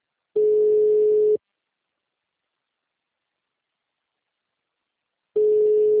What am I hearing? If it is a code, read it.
Ringback tone (Europe, 425 Hz).